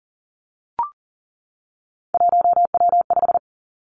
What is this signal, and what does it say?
Morse code, heard as a keyed tone, about 42 wpm, 700 Hz. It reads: 1W5
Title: Three Challenging Letter-Number Combinations 40wpm